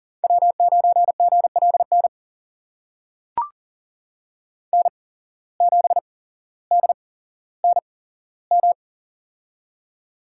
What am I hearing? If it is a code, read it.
Morse code, heard as a keyed tone, about 40 wpm, spaced as for 17 wpm, 700 Hz. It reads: W9GLN N7DNM
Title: USA Call Signs - 5 Characters - 8x Character Spacing - 40wpm